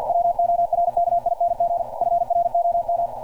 Morse code, heard as a keyed tone, about 50 wpm, 690 Hz. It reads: WORKING FROM
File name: archived / CRSnet